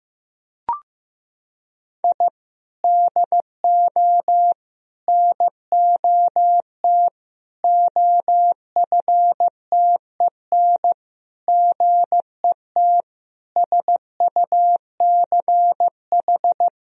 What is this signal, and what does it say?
Morse code, heard as a keyed tone, about 15 wpm, 700 Hz. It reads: I DO NOT OFTEN GET SUCH
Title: Sentences from Top 1000 Words - Review Part 1 - 15wpm